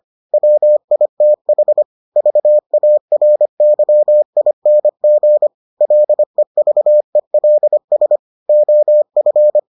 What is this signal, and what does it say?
Morse code, heard as a keyed tone, about 25 wpm, 600 Hz. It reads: WITH VARYING LEVELS OF